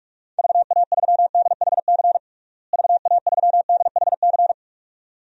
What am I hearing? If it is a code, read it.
Morse code, heard as a keyed tone, about 46 wpm, 700 Hz. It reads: VA3DHC VA3DHC
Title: DXCC-100 Call Signs - 6 Characters - Repeated 2x - 45wpm